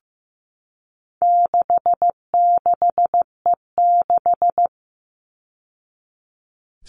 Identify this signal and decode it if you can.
Morse code, heard as a keyed tone, about 15 wpm, 700 Hz. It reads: 66E6
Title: All Cut Numbers - 4-Digit 15wpm